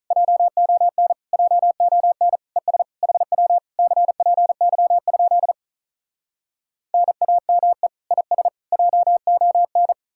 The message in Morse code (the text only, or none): JON JON ES HW CPY? NAME IS JON